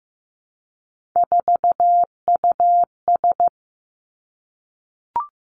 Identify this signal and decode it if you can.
Morse code, heard as a keyed tone, about 15 wpm, 700 Hz. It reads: 4US
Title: Three Challenging Letter-Number Combinations 15wpm